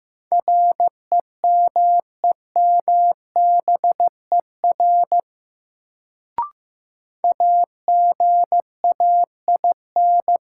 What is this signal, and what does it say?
Morse code, heard as a keyed tone, about 15 wpm, 700 Hz. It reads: REMEMBER AGAIN
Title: Sets of 4 Words - New Word Drill - 15wpm